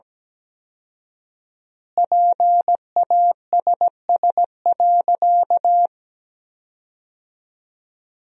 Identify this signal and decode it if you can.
Morse code, heard as a keyed tone, about 17 wpm, 700 Hz. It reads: PASS.
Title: 4x Word-Spacing - Sentences from Top 400 Words - 17wpm